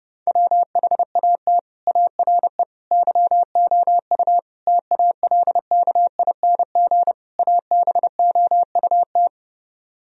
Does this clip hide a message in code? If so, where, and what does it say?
Yes: Morse code from 0.3 to 9.3 s, heard as a keyed tone, about 29 wpm, 700 Hz. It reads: WHAT ARE YOU TALKING ABOUT